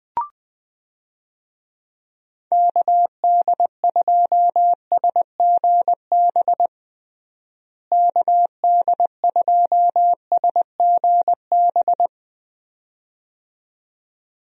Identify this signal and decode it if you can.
Morse code, heard as a keyed tone, about 20 wpm, 700 Hz. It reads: KD2SGB KD2SGB
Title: USA Call Signs - 6 Characters - Repeated 2x - 20wpm